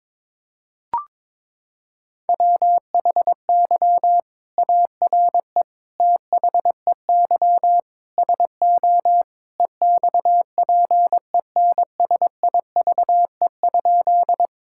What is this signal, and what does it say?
Morse code, heard as a keyed tone, about 22 wpm, 700 Hz. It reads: WHY ARE THEY SO EXPENSIVE?
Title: Sentences from Top 1000 Words - Review Part 6 - 22wpm